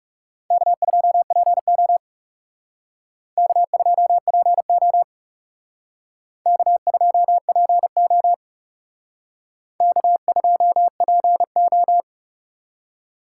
Morse code, heard as a keyed tone, about 36 wpm, 700 Hz. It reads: K2PO K2PO K2PO K2PO
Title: Speed Racing - USA 1x2 Call Signs 30wpm